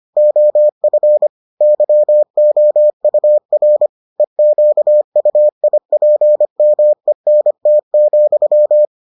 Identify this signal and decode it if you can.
Morse code, heard as a keyed tone, about 25 wpm, 600 Hz. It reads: OF YOUR EQUIPMENT,